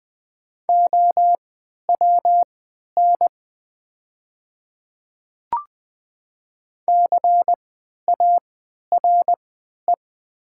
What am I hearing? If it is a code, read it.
Morse code, heard as a keyed tone, about 20 wpm, spaced as for 11 wpm, 700 Hz. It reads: OWN CARE
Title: 3x Character Spacing - Top 500 Words - 20wpm